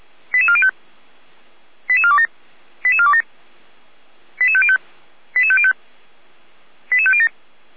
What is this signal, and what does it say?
Five-tone selective call (ZVEI), heard as quick runs of five tones, tones of about 70 ms. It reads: 88405 88427 88427 88505 88505 88507